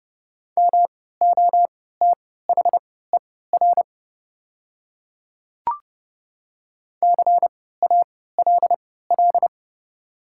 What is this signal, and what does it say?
Morse code, heard as a keyed tone, about 30 wpm, spaced as for 17 wpm, 700 Hz. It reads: MOTHER CALL
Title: 3x Character Spacing - Top 200 Words - 30wpm